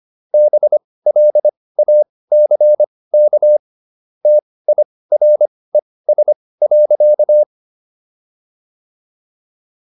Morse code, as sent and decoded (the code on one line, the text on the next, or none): -... .-.. .- -.-. -.- / - .. .-. ... .-.-.-
BLACK TIRES.